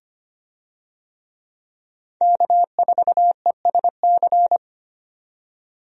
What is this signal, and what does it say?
Morse code, heard as a keyed tone, about 25 wpm, 700 Hz. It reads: K4ESC
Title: USA Call Signs - 5 Characters - 25wpm